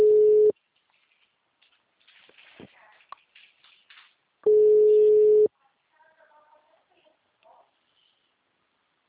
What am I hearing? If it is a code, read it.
Ringback tone (Europe, 425 Hz).
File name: out- -2012- - - .63302